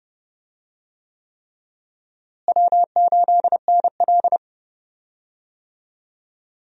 Morse code, heard as a keyed tone, about 32 wpm, 700 Hz. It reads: W8NL
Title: USA Call Signs - 4 Characters - Repeated 3x - 30wpm